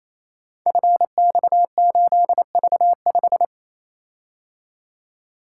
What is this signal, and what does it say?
Morse code, heard as a keyed tone, about 27 wpm, 700 Hz. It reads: FX8V5
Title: Code Groups - 5 Letter:Numbers - 28wpm